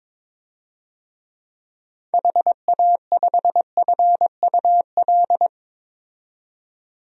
Morse code, heard as a keyed tone, about 22 wpm, 700 Hz. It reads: HA5FUL
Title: DXCC-100 Call Signs - 6 Characters - 22wpm